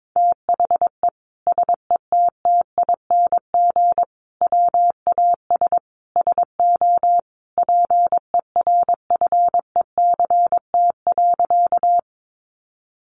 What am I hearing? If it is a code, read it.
Morse code, heard as a keyed tone, about 22 wpm, 700 Hz. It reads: THE SETTING WAS SO PERFECT.
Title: Sentences from Top 1300 Words 22wpm